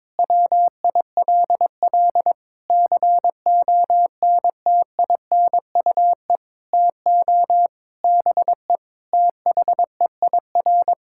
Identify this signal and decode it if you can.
Morse code, heard as a keyed tone, about 22 wpm, 700 Hz. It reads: WILL CONTINUE TO BE THEIR
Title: Sentences from Top 1600 Words 22wpm